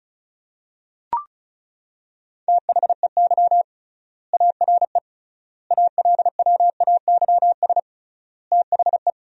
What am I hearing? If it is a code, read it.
Morse code, heard as a keyed tone, about 36 wpm, 700 Hz. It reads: THEY ARE ALWAYS THE
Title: 3x Word-Spacing - Sentences from Top 300 Words - 35wpm